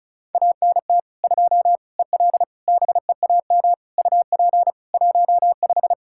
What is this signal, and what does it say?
Morse code, heard as a keyed tone, about 34 wpm, 700 Hz. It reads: ANT 2 EL BEAM UP 15
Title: Practice QSO - Exchange 2 - Info - 35wpm